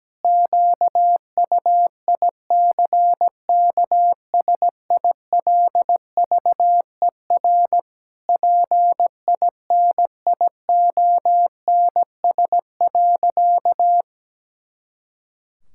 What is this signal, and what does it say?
Morse code, heard as a keyed tone, about 17 wpm, 700 Hz. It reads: QUICKSILVER PINIONS.
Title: Pangram Challenge v2 - 17wpm